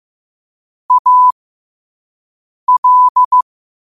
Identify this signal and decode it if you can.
Morse code, heard as a keyed tone, about 15 wpm, spaced as for 5 wpm, 1000 Hz. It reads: AL
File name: qotdfiles / 5wpm